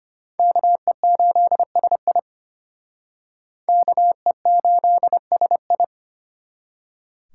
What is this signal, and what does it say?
Morse code, heard as a keyed tone, about 33 wpm, 700 Hz. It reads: KE8SI KE8SI